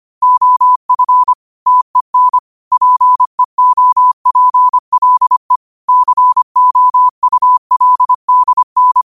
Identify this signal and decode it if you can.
Morse code, heard as a keyed tone, about 25 wpm, 1000 Hz. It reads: OF TEN PEOPLE COULDN
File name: qotdfiles / 25wpm